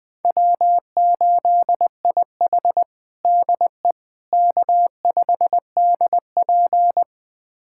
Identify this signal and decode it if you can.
Morse code, heard as a keyed tone, about 20 wpm, 700 Hz. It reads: W8IH DE K5DP
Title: USA Callsign DE Callsign - 4 Characters - 20wpm